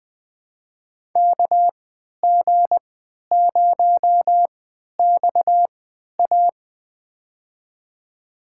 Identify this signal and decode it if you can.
Morse code, heard as a keyed tone, about 20 wpm, spaced as for 11 wpm, 700 Hz. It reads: KG0XA
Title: USA Call Signs - 5 Characters - 3x Character Spacing - 20wpm